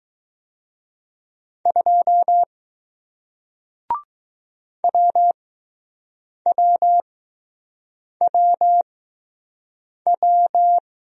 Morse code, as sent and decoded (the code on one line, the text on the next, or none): ..--- / .-- .-- .-- .--
2 WWWW